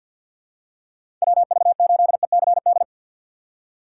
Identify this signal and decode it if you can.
Morse code, heard as a keyed tone, about 48 wpm, 700 Hz. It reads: WV8ECD
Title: USA Call Signs - 6 Characters - 3x Character Spacing - 50wpm